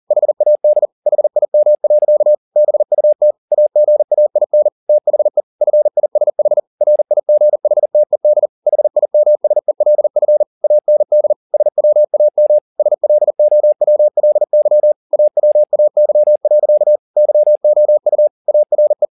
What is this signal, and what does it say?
Morse code, heard as a keyed tone, about 40 wpm, 600 Hz. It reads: HAD HIM. BUT AGAIN THE FISH RIGHTED HIMSELF AND SWAM SLOWLY AWAY. YOU ARE
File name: the old man and the sea 2 40wpm